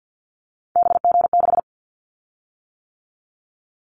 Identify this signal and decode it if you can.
Morse code, heard as a keyed tone, about 50 wpm, 700 Hz. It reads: BG6
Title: Callsign Prefixes for Top 100 DXCC - 50wpm